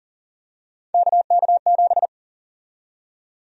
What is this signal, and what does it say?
Morse code, heard as a keyed tone, about 40 wpm, 700 Hz. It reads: KK7
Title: USA Call Sign Prefixes - 40wpm